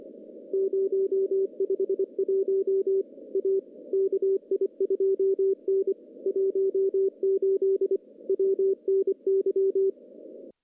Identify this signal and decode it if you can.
Morse code, heard as a keyed tone, about 24 wpm, 390 Hz. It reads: 051 A KI2N 18 WNY
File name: contests / CW-SS-k5tr- -0100